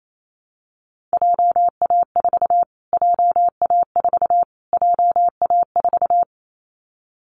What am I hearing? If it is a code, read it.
Morse code, heard as a keyed tone, about 28 wpm, 700 Hz. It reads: JA4 JA4 JA4
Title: Callsign Prefixes for Top 25 DXCC - Repeated 3x - 28wpm